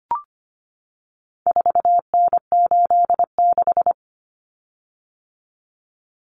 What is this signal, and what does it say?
Morse code, heard as a keyed tone, about 25 wpm, 700 Hz. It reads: 4N86
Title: Common Cut Numbers - 4-Digit 25wpm